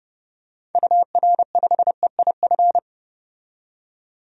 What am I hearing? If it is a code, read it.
Morse code, heard as a keyed tone, about 30 wpm, 700 Hz. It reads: UR5EIF
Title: DXCC-100 Call Signs - 6 Characters - 30wpm